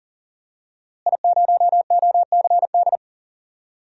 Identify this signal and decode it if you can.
Morse code, heard as a keyed tone, about 38 wpm, 700 Hz. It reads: I0OCD
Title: DXCC-100 Call Signs - 5 Characters - Repeated 2x - 40wpm